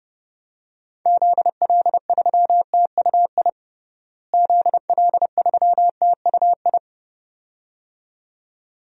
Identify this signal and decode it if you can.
Morse code, heard as a keyed tone, about 30 wpm, 700 Hz. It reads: ZL3TUI ZL3TUI
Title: DXCC-100 Call Signs - 6 Characters - Repeated 2x - 30wpm